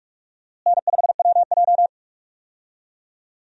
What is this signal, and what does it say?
Morse code, heard as a keyed tone, about 45 wpm, 700 Hz. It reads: N5WJ
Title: USA Call Signs - 4 Characters - Repeated 2x - 45wpm